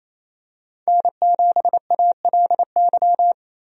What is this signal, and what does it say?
Morse code, heard as a keyed tone, about 28 wpm, 700 Hz. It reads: N7ALY
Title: USA Call Signs - 5 Characters - Repeated 2x - 28wpm